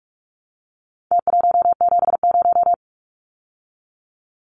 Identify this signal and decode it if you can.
Morse code, heard as a keyed tone, about 44 wpm, 700 Hz. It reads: T170
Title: Numbers - Contest Style - 4-Digit 45wpm